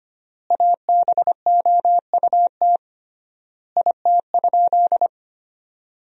Morse code, heard as a keyed tone, about 25 wpm, 700 Hz. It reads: ABOUT IT?
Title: 3x Word-Spacing - Sentences from Top 100 Words - 25wpm